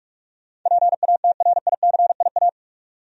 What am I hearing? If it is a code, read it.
Morse code, heard as a keyed tone, about 46 wpm, 700 Hz. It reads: PATRICIA
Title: USA Common Names - 45wpm